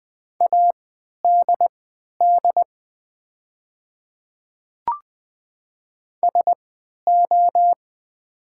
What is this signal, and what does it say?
Morse code, heard as a keyed tone, about 20 wpm, spaced as for 11 wpm, 700 Hz. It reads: ADD SO
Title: 3x Character Spacing - Top 200 Words - 20wpm